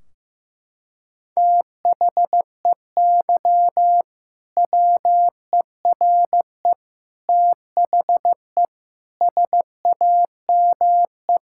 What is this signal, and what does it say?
Morse code, heard as a keyed tone, about 15 wpm, 700 Hz. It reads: THEY WERE THE SAME
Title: Sentences from Top 1600 Words 15wpm